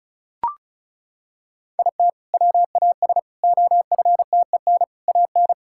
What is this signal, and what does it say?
Morse code, heard as a keyed tone, about 36 wpm, 700 Hz. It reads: IT WAS OFTEN AN
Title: Sentences from Top 1000 Words - Review Part 6 - 35wpm